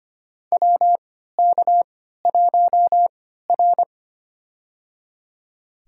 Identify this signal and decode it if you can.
Morse code, heard as a keyed tone, about 25 wpm, spaced as for 14 wpm, 700 Hz. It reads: WK1R
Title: USA Call Signs - 3x Character Spacing - 25wpm